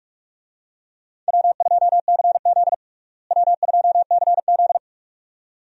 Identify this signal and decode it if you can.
Morse code, heard as a keyed tone, about 45 wpm, 700 Hz. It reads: W2CZ W2CZ
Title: USA Call Signs - 4 Characters - Repeated 2x - 45wpm